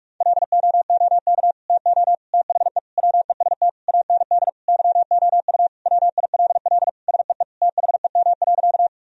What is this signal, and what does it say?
Morse code, heard as a keyed tone, about 46 wpm, 700 Hz. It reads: LOOK TO THE WEST AND YOU WILL SEE THEM.
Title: Sentences from Top 1000 Words - Review Part 2 - 45wpm